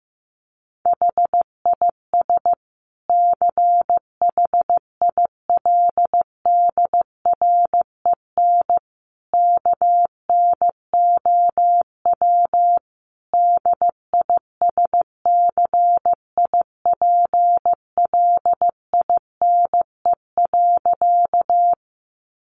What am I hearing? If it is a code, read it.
Morse code, heard as a keyed tone, about 14 wpm, 700 Hz. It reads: HIS CHILDREN KNOW DISCIPLINE.
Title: Sentences from Top 900 Words 15wpm